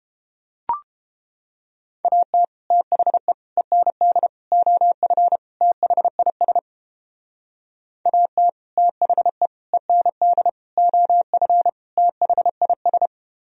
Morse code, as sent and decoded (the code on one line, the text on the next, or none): .- - / - .... / . -. -.. / --- ..-. / - .... .. ... / .- - / - .... / . -. -.. / --- ..-. / - .... .. ...
AT THE END OF THIS AT THE END OF THIS